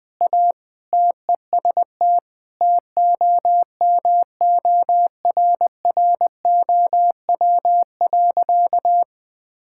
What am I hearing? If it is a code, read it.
Morse code, heard as a keyed tone, about 20 wpm, 700 Hz. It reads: A TEST TOMORROW.
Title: Sentences with 3-Word Phrases - 20wpm